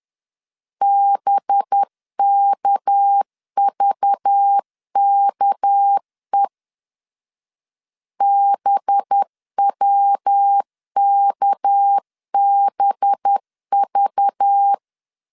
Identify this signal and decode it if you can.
Morse code, heard as a keyed tone, about 10 wpm, 800 Hz. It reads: BKVKE BWKBV
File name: LEZIONI / 6 PKX0 / 6 PK 10wpm 2436